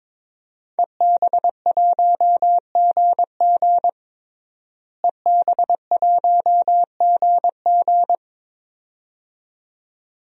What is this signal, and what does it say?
Morse code, heard as a keyed tone, about 22 wpm, 700 Hz. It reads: EB1GG EB1GG
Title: DXCC-100 Call Signs - 5 Characters - Repeated 2x - 22wpm